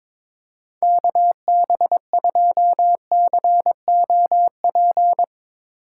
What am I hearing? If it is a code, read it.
Morse code, heard as a keyed tone, about 22 wpm, 700 Hz. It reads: KB2COP
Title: Speed Racing - USA 2x3 Call Signs 22wpm